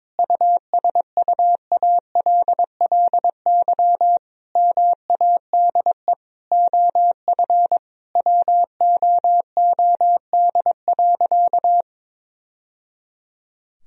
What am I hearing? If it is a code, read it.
Morse code, heard as a keyed tone, about 22 wpm, 700 Hz. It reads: USUALLY MADE OF WOOD.
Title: Sentences from Top 1000 Words - Review Part 3 - 22wpm